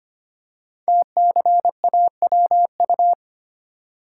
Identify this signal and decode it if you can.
Morse code, heard as a keyed tone, about 25 wpm, 700 Hz. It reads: TCAWU
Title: Code Groups - 5 Letter:Numbers - 25wpm